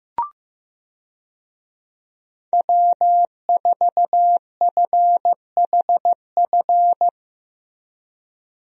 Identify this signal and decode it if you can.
Morse code, heard as a keyed tone, about 15 wpm, 700 Hz. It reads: W4FHF
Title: USA Call Signs - 5 Characters - Repeated 3x - 15wpm